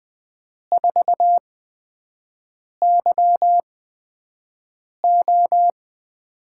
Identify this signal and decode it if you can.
Morse code, heard as a keyed tone, about 20 wpm, spaced as for 5 wpm, 700 Hz. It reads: 4YO
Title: USA Call Signs - 8x Character Spacing - 20wpm